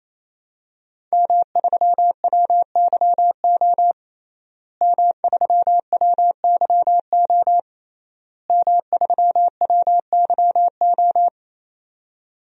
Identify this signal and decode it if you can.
Morse code, heard as a keyed tone, about 27 wpm, 700 Hz. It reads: M3WYO M3WYO M3WYO
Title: DXCC-100 Call Signs - 5 Characters - Repeated 3x - 28wpm